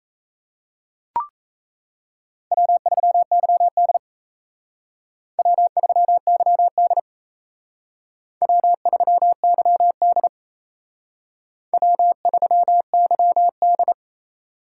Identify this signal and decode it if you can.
Morse code, heard as a keyed tone, about 36 wpm, 700 Hz. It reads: W3YD W3YD W3YD W3YD